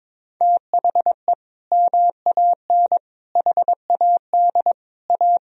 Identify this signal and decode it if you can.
Morse code, heard as a keyed tone, about 22 wpm, 700 Hz. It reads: THE MAN HAD A
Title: Sentences from Top 1000 Words - Review Part 2 - 22wpm